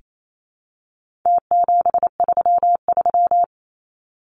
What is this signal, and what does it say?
Morse code, heard as a keyed tone, about 28 wpm, 700 Hz. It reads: T733
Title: Numbers - Contest Style - 4-Digit 28wpm